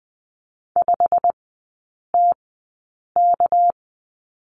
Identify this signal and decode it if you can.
Morse code, heard as a keyed tone, about 20 wpm, spaced as for 8 wpm, 700 Hz. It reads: 5TK
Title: 198 Lesson 32e - Letter k - Three Characters 20wpm